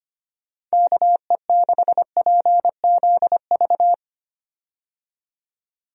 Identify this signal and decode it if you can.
Morse code, heard as a keyed tone, about 25 wpm, 700 Hz. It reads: KE6PZV
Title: USA Call Signs - 6 Characters - 25wpm